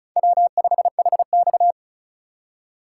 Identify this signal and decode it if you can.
Morse code, heard as a keyed tone, about 36 wpm, 700 Hz. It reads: W5HX
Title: USA Call Signs - 4 Characters - 3x Character Spacing - 35wpm